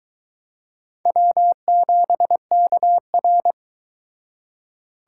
Morse code, heard as a keyed tone, about 22 wpm, 700 Hz. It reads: W7KR